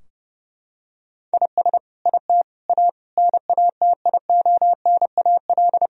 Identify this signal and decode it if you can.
Morse code, heard as a keyed tone, about 30 wpm, 700 Hz. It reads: IS IT A NATIONAL
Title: Sentences from Top 1000 Words - Review Part 4 - 30wpm